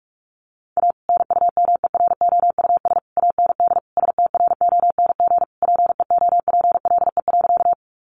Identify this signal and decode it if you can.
Morse code, heard as a keyed tone, about 44 wpm, 700 Hz. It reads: A NUMEROUS AND STRONG PEOPLE.
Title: Sentences from Top 1400 Words 45wpm